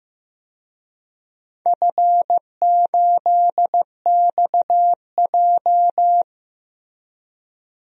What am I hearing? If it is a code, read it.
Morse code, heard as a keyed tone, about 16 wpm, 700 Hz. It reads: F8XJ
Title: DXCC-100 Call Signs - 4 Characters - Repeated 2x - 15wpm